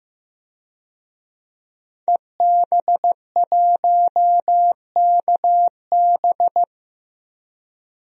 Morse code, heard as a keyed tone, about 15 wpm, 700 Hz. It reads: EB1KB
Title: DXCC-100 Call Signs - 5 Characters - Repeated 3x - 15wpm